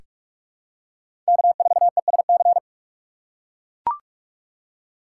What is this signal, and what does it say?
Morse code, heard as a keyed tone, about 41 wpm, 700 Hz. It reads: K4ESC
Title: USA Call Signs - 5 Characters - 3x Character Spacing - 45wpm